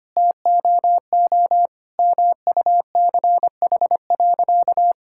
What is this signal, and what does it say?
Morse code, heard as a keyed tone, about 25 wpm, 700 Hz. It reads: TOO MUCH.
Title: Sentences from Top 1000 Words - Review Part 5 - 25wpm